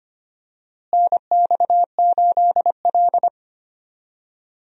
Morse code, heard as a keyed tone, about 25 wpm, 700 Hz. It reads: NX8L
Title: USA Call Signs - 4 Characters - Repeated 2x - 25wpm